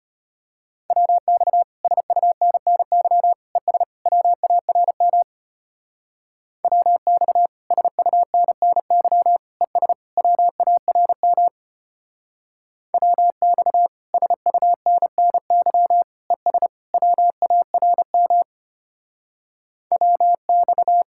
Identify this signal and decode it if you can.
Morse code, heard as a keyed tone, about 30 wpm, 700 Hz. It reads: WX SUNNY ES WARM WX SUNNY ES WARM WX SUNNY ES WARM WX
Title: Speed-Racing Common QSO Phrases Encore - 25wpm